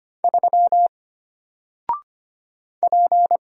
Morse code, heard as a keyed tone, about 25 wpm, 700 Hz. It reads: 3 P
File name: Single-Letter-Number-25wpm